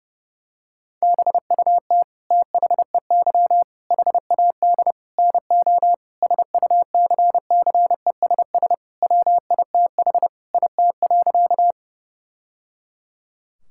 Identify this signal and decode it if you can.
Morse code, heard as a keyed tone, about 32 wpm, 700 Hz. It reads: BUT THEY HAD NO SUCCESS WITH IT.
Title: Sentences from Top 1000 Words - Review Part 6 - 30wpm